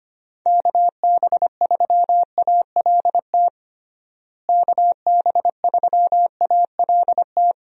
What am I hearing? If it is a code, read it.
Morse code, heard as a keyed tone, about 25 wpm, 700 Hz. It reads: KB3ALT KB3ALT
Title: USA Call Signs - 6 Characters - Repeated 3x - 25wpm